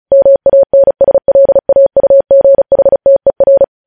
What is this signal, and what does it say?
Morse code, heard as a keyed tone, about 36 wpm, 550 Hz. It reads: MANSLAUGHTER
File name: podcast / News headlines-1x-35x35